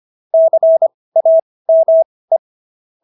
Morse code, heard as a keyed tone, about 25 wpm, spaced as for 18 wpm, 650 Hz. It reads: CAME